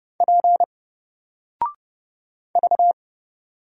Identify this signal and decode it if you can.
Morse code, heard as a keyed tone, about 30 wpm, 700 Hz. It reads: P V
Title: Single Letter-Number 30wpm